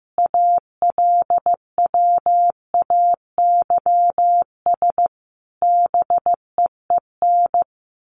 Morse code, heard as a keyed tone, about 15 wpm, 700 Hz. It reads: ALWAYS BEEN
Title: Sentences from Top 1400 Words 15wpm